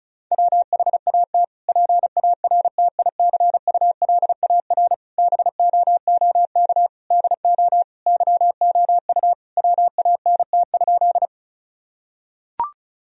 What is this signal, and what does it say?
Morse code, heard as a keyed tone, about 34 wpm, 700 Hz. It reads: WHAT PARTICULAR BOOK DO YOU WANT?
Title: Sentences from Top 1000 Words - Review Part 4 - 35wpm